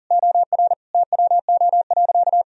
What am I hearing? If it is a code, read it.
Morse code, heard as a keyed tone, about 40 wpm, 700 Hz. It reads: OR TWO.
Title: Sentences from Top 1000 Words - Review Part 1 - 40wpm